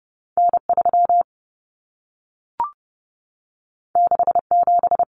Morse code, heard as a keyed tone, about 30 wpm, 700 Hz. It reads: N3 67
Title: Common Cut Numbers - 2-Digit 30wpm